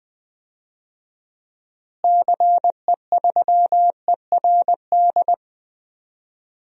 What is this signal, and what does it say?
Morse code, heard as a keyed tone, about 20 wpm, 700 Hz. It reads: CE3ERD
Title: DXCC-100 Call Signs - 6 Characters - Repeated 2x - 20wpm